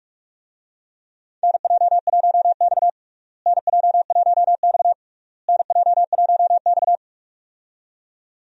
Morse code, heard as a keyed tone, about 46 wpm, 700 Hz. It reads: NJ1X NJ1X NJ1X
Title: USA Call Signs - 4 Characters - Repeated 3x - 45wpm